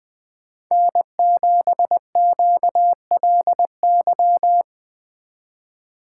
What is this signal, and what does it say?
Morse code, heard as a keyed tone, about 20 wpm, 700 Hz. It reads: N7QLY